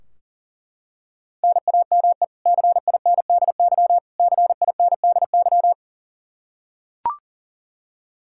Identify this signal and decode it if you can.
Morse code, heard as a keyed tone, about 40 wpm, 700 Hz. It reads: NAME CINDY CINDY
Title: K1USN SST QSO Practice 40wpm